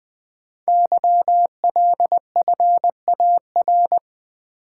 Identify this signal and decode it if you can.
Morse code, heard as a keyed tone, about 20 wpm, 700 Hz. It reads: YLFAR
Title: Code Groups - 5 Letters - 20wpm